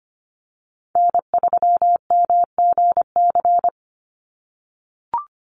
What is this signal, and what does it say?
Morse code, heard as a keyed tone, about 25 wpm, 690 Hz. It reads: N3MGC
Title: USA Call Signs 25wpm